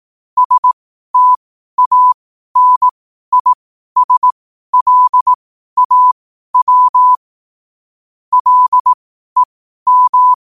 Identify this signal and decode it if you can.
Morse code, heard as a keyed tone, about 18 wpm, spaced as for 12 wpm, 1000 Hz. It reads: STANISLAW LEM
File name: qotdfiles / 12wpm